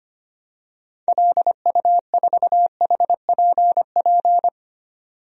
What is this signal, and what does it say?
Morse code, heard as a keyed tone, about 25 wpm, 700 Hz. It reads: LU4HPP
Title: DXCC-100 Call Signs - 6 Characters - 25wpm